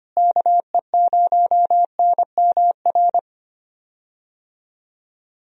Morse code, heard as a keyed tone, about 25 wpm, 700 Hz. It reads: KE0NMR
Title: USA Call Signs - 6 Characters - Repeated 2x - 25wpm